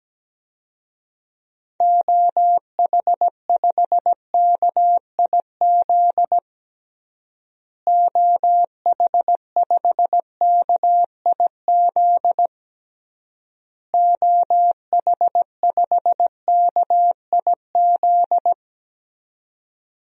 Morse code, heard as a keyed tone, about 17 wpm, 700 Hz. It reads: OH5KIZ OH5KIZ OH5KIZ
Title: DXCC-100 Call Signs - 6 Characters - Repeated 3x - 17wpm